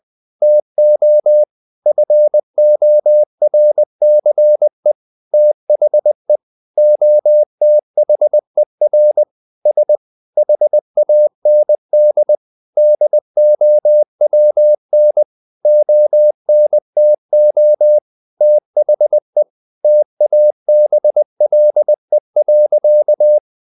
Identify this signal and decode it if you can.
Morse code, heard as a keyed tone, about 20 wpm, 600 Hz. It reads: TO FORCE THE OTHER S HAND DOWN ONTO THE TABLE.
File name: the old man and the sea / the old man and the sea part 1 20wpm